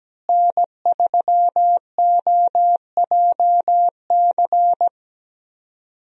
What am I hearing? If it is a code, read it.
Morse code, heard as a keyed tone, about 17 wpm, 700 Hz. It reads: N3OJC
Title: USA Call Signs - 5 Characters - Repeated 2x - 17wpm